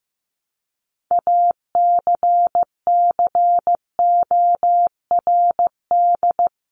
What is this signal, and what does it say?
Morse code, heard as a keyed tone, about 15 wpm, 700 Hz. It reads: ACCORD
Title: ING Suffix - Top 100 Words with Base Words 15wpm